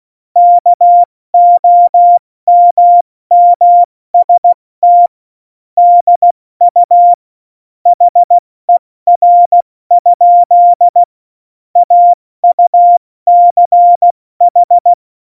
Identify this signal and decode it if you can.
Morse code, heard as a keyed tone, about 16 wpm, 700 Hz. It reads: KOMMST DU HER? AUCH
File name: Morsekurs / Witze / Witz 19. Februar 2019, 70 F80